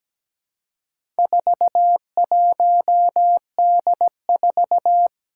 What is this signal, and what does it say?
Morse code, heard as a keyed tone, about 17 wpm, 700 Hz. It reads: 41D4
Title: Four Challenging Letter-Number Combinations 17wpm